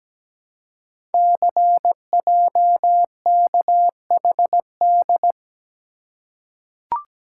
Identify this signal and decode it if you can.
Morse code, heard as a keyed tone, about 17 wpm, 700 Hz. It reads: CJKHD
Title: Code Groups - 5 Letter:Numbers - 17wpm